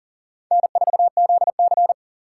Morse code, heard as a keyed tone, about 40 wpm, 700 Hz. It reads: N4ZC
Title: USA Call Signs - 4 Characters - 4x Character Spacing - 40wpm